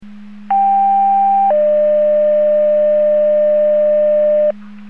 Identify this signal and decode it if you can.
Two-tone page, heard as a one-second tone, then a three-second tone, about 790 then 600 Hz, with noise. It is heard from 0.5 to 4.5 s.